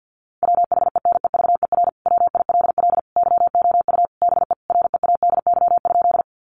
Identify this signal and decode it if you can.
Morse code, heard as a keyed tone, about 50 wpm, 700 Hz. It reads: WHENEVER WILL YOU BE READY?
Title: Sentences from Top 1200 Words 50wpm